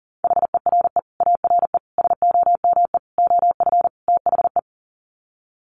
Morse code, heard as a keyed tone, about 40 wpm, 700 Hz. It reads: HERE ARE SOME OF THE
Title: Sets of 5 Words Encore 40wpm